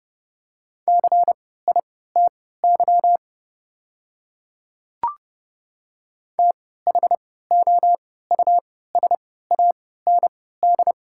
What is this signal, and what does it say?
Morse code, heard as a keyed tone, about 30 wpm, spaced as for 17 wpm, 700 Hz. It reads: CITY THOUSAND
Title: 3x Character Spacing - Top 500 Words - 30wpm